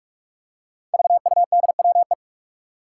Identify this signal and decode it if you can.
Morse code, heard as a keyed tone, about 47 wpm, 700 Hz. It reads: VUDWE
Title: Five Challenging Letter-Number Combinations 45wpm